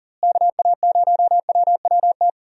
Morse code, heard as a keyed tone, about 40 wpm, 700 Hz. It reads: KA0WWT
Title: USA Call Signs - 6 Characters - Repeated 3x - 40wpm